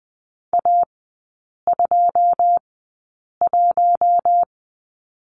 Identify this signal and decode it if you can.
Morse code, heard as a keyed tone, about 19 wpm, spaced as for 8 wpm, 700 Hz. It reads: A21
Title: 121 Lesson 20e - Number 3 - Three Characters 20wpm